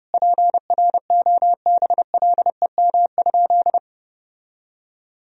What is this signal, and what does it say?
Morse code, heard as a keyed tone, about 30 wpm, 700 Hz. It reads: PROBLEM?
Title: Sentences from Top 1000 Words - Review Part 4 - 30wpm